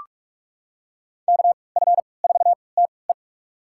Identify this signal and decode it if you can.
Morse code, heard as a keyed tone, about 41 wpm, spaced as for 25 wpm, 700 Hz. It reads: KF4TE
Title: USA Call Signs - 5 Characters - 3x Character Spacing - 45wpm